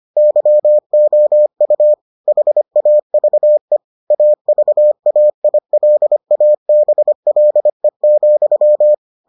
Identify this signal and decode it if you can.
Morse code, heard as a keyed tone, about 25 wpm, 600 Hz. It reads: YOU HAVE AVAILABLE,